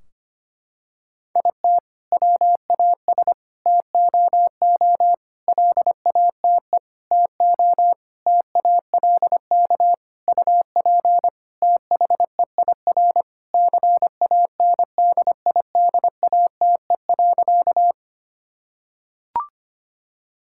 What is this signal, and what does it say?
Morse code, heard as a keyed tone, about 25 wpm, 700 Hz. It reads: IT WAS TOO LATE TO TALK UP THEIR CANDIDATE.
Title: Sentences from Top 1600 Words 25wpm